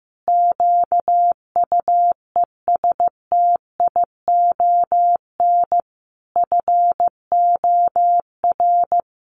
Morse code, heard as a keyed tone, about 15 wpm, 700 Hz. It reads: QUESTION FOR
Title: Sentences from Top 600 Words 15wpm